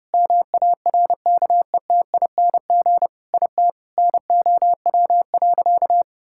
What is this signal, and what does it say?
Morse code, heard as a keyed tone, about 30 wpm, 700 Hz. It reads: MARKETING IT NOW.
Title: Sentences from Top 1000 Words - Review Part 5 - 30wpm